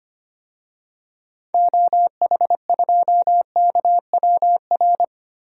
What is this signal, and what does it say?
Morse code, heard as a keyed tone, about 25 wpm, 700 Hz. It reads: OH2KWR
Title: DXCC-100 Call Signs - 6 Characters - Repeated 2x - 25wpm